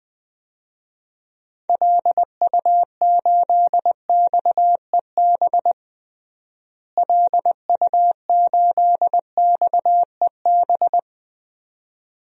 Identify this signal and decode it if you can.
Morse code, heard as a keyed tone, about 21 wpm, 700 Hz. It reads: LU8XEB LU8XEB
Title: DXCC-100 Call Signs - 6 Characters - Repeated 2x - 20wpm